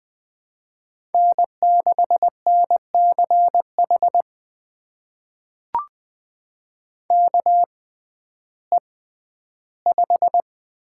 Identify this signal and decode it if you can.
Morse code, heard as a keyed tone, about 20 wpm, spaced as for 12 wpm, 700 Hz. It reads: N6NCH KE5
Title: USA Call Signs - 6x Character Spacing - 20wpm